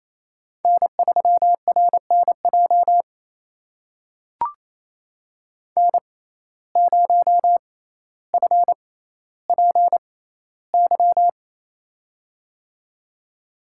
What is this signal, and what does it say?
Morse code, heard as a keyed tone, about 28 wpm, spaced as for 14 wpm, 700 Hz. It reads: N3RNJ N0FPY
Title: USA Call Signs - 5 Characters - 6x Character Spacing - 28wpm